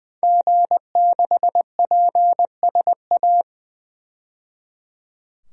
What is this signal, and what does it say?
Morse code, heard as a keyed tone, about 20 wpm, 700 Hz. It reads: G6PSA
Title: DXCC-100 Call Signs - 5 Characters - 20wpm